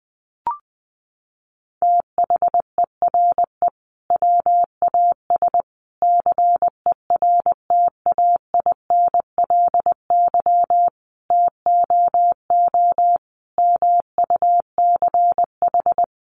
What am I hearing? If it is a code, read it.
Morse code, heard as a keyed tone, about 20 wpm, 700 Hz. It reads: THERE WAS CERTAINLY TOO MUCH
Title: Sentences from Top 900 Words 20wpm